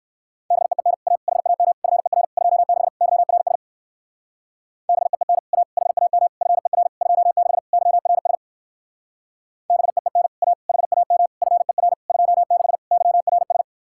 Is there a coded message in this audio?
Yes: Morse code from 0.5 to 13.6 s, heard as a keyed tone, about 58 wpm, 700 Hz. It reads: BEEN A HAM FER 26 YRS BEEN A HAM FER 26 YRS BEEN A HAM FER 26 YRS